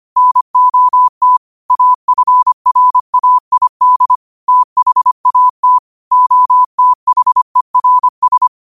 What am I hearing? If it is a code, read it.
Morse code, heard as a keyed tone, about 25 wpm, 1000 Hz. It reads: NOT AFRAID THAT OTHERS
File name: qotdfiles / 25wpm